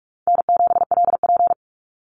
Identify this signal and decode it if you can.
Morse code, heard as a keyed tone, about 44 wpm, 700 Hz. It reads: N7LP
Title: NA SOTA Chaser Callsigns 45wpm